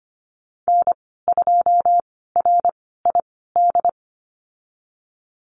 Morse code, heard as a keyed tone, about 25 wpm, spaced as for 16 wpm, 700 Hz. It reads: N2RID
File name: Lesson 15g - Number 5 - Call Signs 25wpm